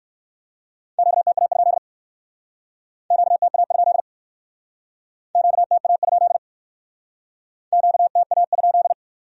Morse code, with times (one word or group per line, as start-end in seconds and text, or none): 1.0-1.8 QTA?
3.1-4.0 QTA?
5.3-6.4 QTA?
7.7-8.9 QTA?